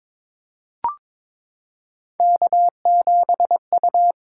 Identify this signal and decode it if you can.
Morse code, heard as a keyed tone, about 22 wpm, 700 Hz. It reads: K7U
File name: practice / 22wpm / Three Letter-Number Combinations - 22wpm